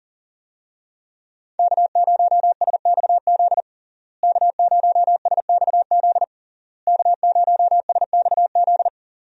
Morse code, heard as a keyed tone, about 40 wpm, 700 Hz. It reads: K0SXZ K0SXZ K0SXZ
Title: USA Call Signs - 5 Characters - Repeated 3x - 40wpm